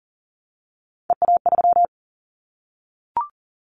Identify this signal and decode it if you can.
Morse code, heard as a keyed tone, about 37 wpm, 700 Hz. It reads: EA3
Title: Callsign Prefixes for Top 25 DXCC - 40wpm